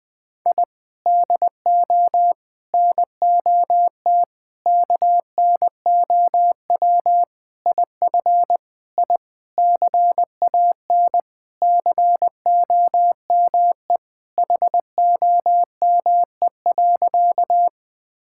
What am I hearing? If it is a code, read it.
Morse code, heard as a keyed tone, about 20 wpm, 700 Hz. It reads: I DO NOT KNOW IF I CAN COME HOME.
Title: Sentences with 5-Word Phrases - Spoken Beginning - 20wpm